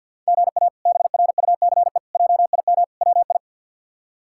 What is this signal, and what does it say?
Morse code, heard as a keyed tone, about 47 wpm, 700 Hz. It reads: GA BRUCE JIM WI
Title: K1USN SST QSO Practice - 50wpm